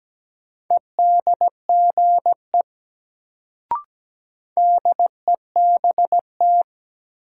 Morse code, heard as a keyed tone, about 17 wpm, 700 Hz. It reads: EDGE DEBT
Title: Sentences from Top 1300 Words - New Word Drill - 17wpm